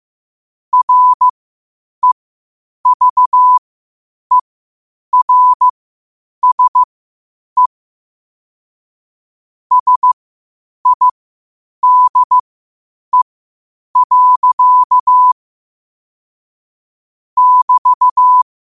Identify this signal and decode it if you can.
Morse code, heard as a keyed tone, about 15 wpm, spaced as for 8 wpm, 1000 Hz. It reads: REVERSE SIDE. <BT>
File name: qotdfiles / 8wpm